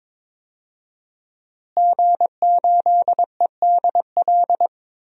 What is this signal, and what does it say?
Morse code, heard as a keyed tone, about 22 wpm, 700 Hz. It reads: G8EDL
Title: DXCC-100 Call Signs - 5 Characters - Repeated 2x - 22wpm